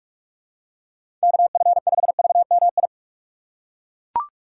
Morse code, heard as a keyed tone, about 41 wpm, 700 Hz. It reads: KF5VMI
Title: USA Call Signs - 6 Characters - 4x Character Spacing - 45wpm